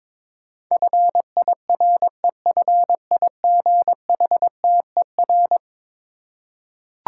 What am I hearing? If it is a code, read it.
Morse code, heard as a keyed tone, about 22 wpm, 700 Hz. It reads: FIREFIGHTER
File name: practice / 22wpm / Eleven Letter Words - 22wpm